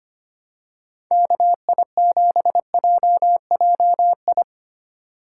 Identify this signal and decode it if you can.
Morse code, heard as a keyed tone, about 25 wpm, 690 Hz. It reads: KI7JJI
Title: USA Call Signs - 6 Characters - 2x Character Spacing - 25wpm